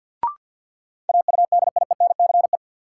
Morse code, heard as a keyed tone, about 51 wpm, 700 Hz. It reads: AUDIENCE
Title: Sentences from Top 800 Words - New Word Drill - 50wpm